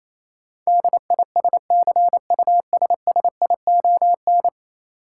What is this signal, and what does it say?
Morse code, heard as a keyed tone, about 28 wpm, 700 Hz. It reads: DISCUSSION
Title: Sentences from Top 1100 Words - New Word Drill - 28wpm